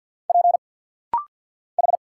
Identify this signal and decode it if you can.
Morse code, heard as a keyed tone, about 50 wpm, 700 Hz. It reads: P H
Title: Single Letters - Rapid-Fire 50wpm